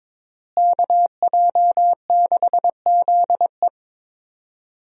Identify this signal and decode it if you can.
Morse code, heard as a keyed tone, about 22 wpm, 700 Hz. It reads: KJ6ZE
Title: USA Call Signs - 5 Characters - Repeated 3x - 22wpm